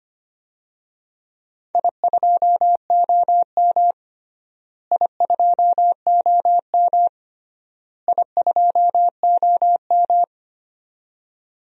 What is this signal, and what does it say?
Morse code, heard as a keyed tone, about 25 wpm, 700 Hz. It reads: I2OM I2OM I2OM
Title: DXCC-100 Call Signs - 4 Characters - Repeated 3x - 25wpm